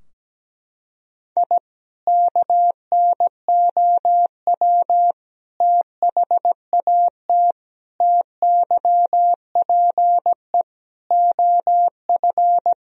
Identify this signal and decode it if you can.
Morse code, heard as a keyed tone, about 17 wpm, 700 Hz. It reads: I KNOW THAT TYPE OF
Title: Sentences from Top 1000 Words - Review Part 3 - 17wpm